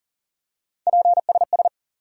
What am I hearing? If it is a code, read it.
Morse code, heard as a keyed tone, about 40 wpm, 700 Hz. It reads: PSS
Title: USA Call Sign Postfixes - 3 Letters 40wpm